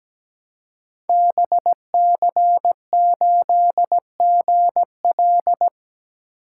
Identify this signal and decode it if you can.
Morse code, heard as a keyed tone, about 17 wpm, 700 Hz. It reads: BC8GL
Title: Code Groups - 5 Letter:Numbers - 17wpm